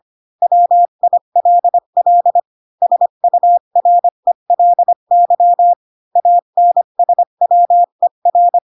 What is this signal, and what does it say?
Morse code, heard as a keyed tone, about 25 wpm, 700 Hz. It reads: WILL SURELY ANSWER